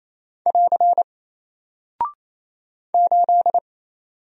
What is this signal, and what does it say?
Morse code, heard as a keyed tone, about 28 wpm, 700 Hz. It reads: <AR> 8
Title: Single Character (Letter:Number:Punctuation:Prosign) Alt 28wpm